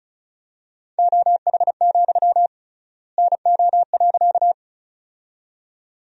Morse code, heard as a keyed tone, about 36 wpm, 700 Hz. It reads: OH, NO.
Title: 3x Word-Spacing - Sentences from Top 500 Words - 35wpm